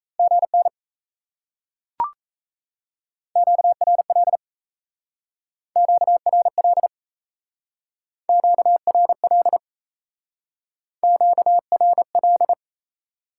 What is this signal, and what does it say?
Morse code, heard as a keyed tone, about 37 wpm, 700 Hz. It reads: GN QRL QRL QRL QRL